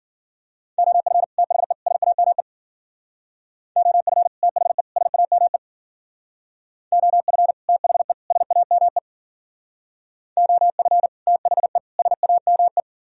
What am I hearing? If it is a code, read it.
Morse code, heard as a keyed tone, about 53 wpm, 700 Hz. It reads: OF THE SAME OF THE SAME OF THE SAME OF THE SAME